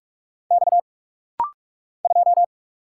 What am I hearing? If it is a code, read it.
Morse code, heard as a keyed tone, about 45 wpm, 700 Hz. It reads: X 2
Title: Single Letter-Number Alt - Rapid-Fire 45wpm